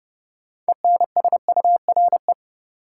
Morse code, heard as a keyed tone, about 30 wpm, 700 Hz. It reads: ENSURE
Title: Sentences from Top 1100 Words - New Word Drill Alt - 30wpm